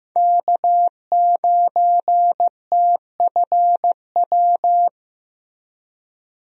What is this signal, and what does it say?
Morse code, heard as a keyed tone, about 15 wpm, 700 Hz. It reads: K9TFW
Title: USA Call Signs - 5 Characters - 15wpm